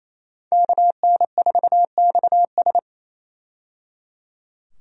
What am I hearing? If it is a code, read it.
Morse code, heard as a keyed tone, about 28 wpm, 700 Hz. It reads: KN4XS
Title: USA Call Signs - 5 Characters - 28wpm